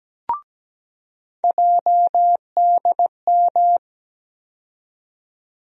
Morse code, heard as a keyed tone, about 17 wpm, 700 Hz. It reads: JDM
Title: USA Call Sign Postfixes - 3 Letters 17wpm